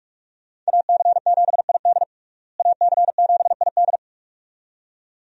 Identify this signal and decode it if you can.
Morse code, heard as a keyed tone, about 46 wpm, 700 Hz. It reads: AC7ID AC7ID
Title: USA Call Signs - 5 Characters - Repeated 2x - 45wpm